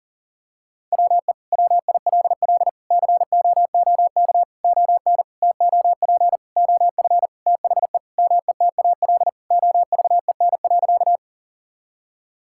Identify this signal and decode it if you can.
Morse code, heard as a keyed tone, about 40 wpm, 700 Hz. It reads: WE WILL COOK ON TOP OF THE METAL OVEN.
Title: Sentences from Top 1000 Words - Review Part 6 - 40wpm